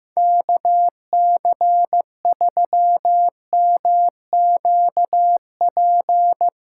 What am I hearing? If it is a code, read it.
Morse code, heard as a keyed tone, about 15 wpm, 700 Hz. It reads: KC3MQP
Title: USA Callsign DE Callsign 15wpm